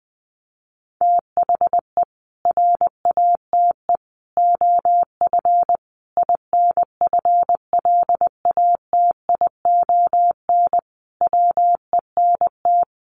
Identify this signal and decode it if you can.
Morse code, heard as a keyed tone, about 20 wpm, 700 Hz. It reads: THE RATE OF INFLATION WENT
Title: Sentences from Top 1100 Words 20wpm